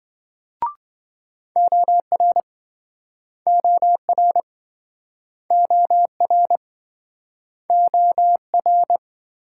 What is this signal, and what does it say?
Morse code, heard as a keyed tone, about 27 wpm, 700 Hz. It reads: OR OR OR OR